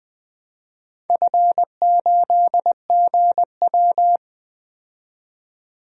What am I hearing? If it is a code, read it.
Morse code, heard as a keyed tone, about 20 wpm, 700 Hz. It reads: F8GW
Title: DXCC-100 Call Signs - 4 Characters - 20wpm